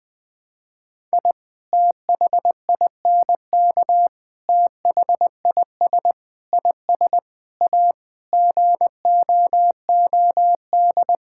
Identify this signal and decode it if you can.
Morse code, heard as a keyed tone, about 20 wpm, 700 Hz. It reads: I THINK THIS IS A GOOD
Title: Sentences with 5-Word Phrases - 20wpm